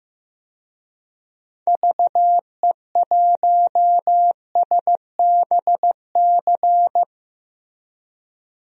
Morse code, heard as a keyed tone, about 16 wpm, 700 Hz. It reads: VE1SBC
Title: DXCC-100 Call Signs - 6 Characters - Repeated 2x - 15wpm